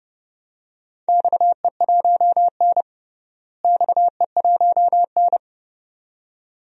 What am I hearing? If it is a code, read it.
Morse code, heard as a keyed tone, about 32 wpm, 700 Hz. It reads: XE1N XE1N
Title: DXCC-100 Call Signs - 4 Characters - Repeated 2x - 30wpm